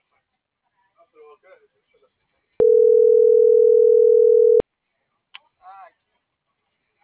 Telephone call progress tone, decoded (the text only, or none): ringback tone